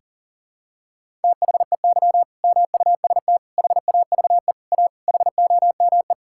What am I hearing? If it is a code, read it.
Morse code, heard as a keyed tone, about 40 wpm, 700 Hz. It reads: THEY MUST HAVE A HOME
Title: Sentences from Top 1000 Words - Review Part 5 - 40wpm